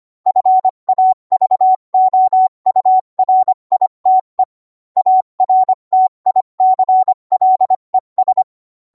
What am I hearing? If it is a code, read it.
Morse code, heard as a keyed tone, about 25 wpm, 750 Hz. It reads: FAVOURITE ARTICLES